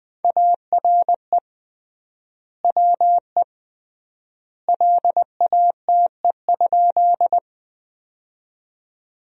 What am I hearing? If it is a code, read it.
Morse code, heard as a keyed tone, about 20 wpm, 700 Hz. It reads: ARE WE LATE?
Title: 3x Word-Spacing - Sentences from Top 300 Words - 20wpm